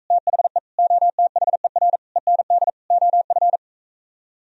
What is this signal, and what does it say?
Morse code, heard as a keyed tone, about 45 wpm, 700 Hz. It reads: THE OTHER END OF